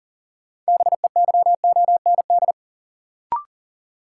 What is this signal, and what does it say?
Morse code, heard as a keyed tone, about 38 wpm, 700 Hz. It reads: BEYOND
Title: Sentences from Top 1000 Words - Review Part 6 - New Word Drill - 40wpm